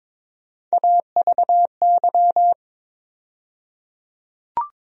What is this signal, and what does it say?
Morse code, heard as a keyed tone, about 22 wpm, 700 Hz. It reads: AVY
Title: USA Call Sign Postfixes - 3 Letters 22wpm